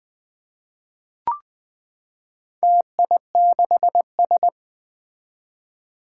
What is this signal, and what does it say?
Morse code, heard as a keyed tone, about 20 wpm, 700 Hz. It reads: TI6S
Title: Four Challenging Letter-Number Combinations 20wpm